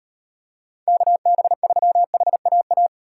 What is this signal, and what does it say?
Morse code, heard as a keyed tone, about 38 wpm, 700 Hz. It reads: KB3HAA